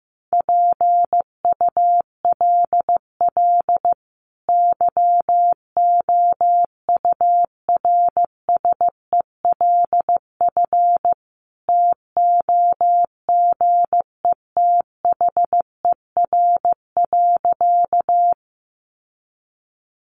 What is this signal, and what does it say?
Morse code, heard as a keyed tone, about 15 wpm, 700 Hz. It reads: PULL YOURSELF TOGETHER.
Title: Sentences from Top 600 Words 15wpm